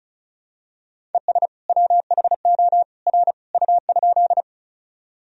Morse code, heard as a keyed tone, about 34 wpm, 700 Hz. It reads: ES WHO R U?
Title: Sentences from Top 100 Words and Abbreviations 35wpm